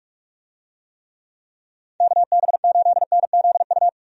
Morse code, heard as a keyed tone, about 41 wpm, 700 Hz. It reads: KB8NZU
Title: USA Call Signs - 6 Characters - 45wpm